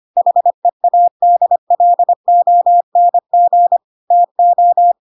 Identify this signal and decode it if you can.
Morse code, heard as a keyed tone, about 25 wpm, 700 Hz. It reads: HEADLONG TO
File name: Pinocchio / Pinocchio ch7 25 25 ik1hge